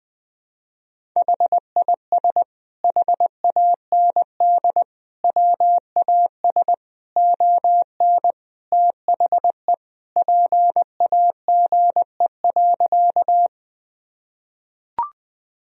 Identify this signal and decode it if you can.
Morse code, heard as a keyed tone, about 20 wpm, 700 Hz. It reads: HIS HAND WAS ON THE PAGE.
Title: Sentences from Top 1000 Words - Review Part 1 - 20wpm